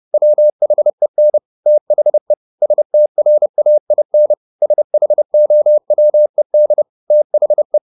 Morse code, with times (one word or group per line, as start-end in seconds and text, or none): 0.1-1.4 WHEN
1.7-2.3 THE
2.6-4.3 STRAIN
4.6-6.8 SHOWED
7.1-7.8 THE